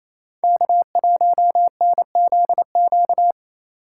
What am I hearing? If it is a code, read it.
Morse code, heard as a keyed tone, about 28 wpm, 700 Hz. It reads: K1NZQ
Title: USA Call Signs - 5 Characters - 6x Character Spacing - 28wpm